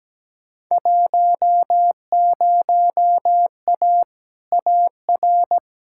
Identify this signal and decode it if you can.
Morse code, heard as a keyed tone, about 17 wpm, 700 Hz. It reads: 10A AR
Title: Field Day Exchanges Encore - 17wpm